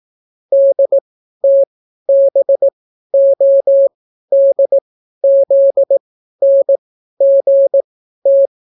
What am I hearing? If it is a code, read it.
Morse code, heard as a keyed tone, about 18 wpm, spaced as for 12 wpm, 550 Hz. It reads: DTBODZNGT